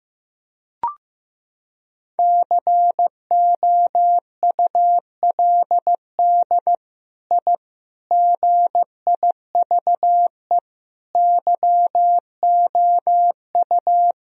Morse code, heard as a keyed tone, about 16 wpm, 700 Hz. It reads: COULD I GIVE YOU
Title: Sentences from Top 1000 Words - Review Part 1 - 15wpm